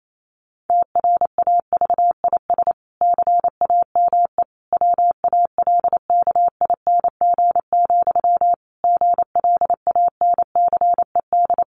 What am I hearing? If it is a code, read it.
Morse code, heard as a keyed tone, about 28 wpm, 700 Hz. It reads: TRAVIS CAME WALKING, GLANCED
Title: A Sound Of Thunder 28wpm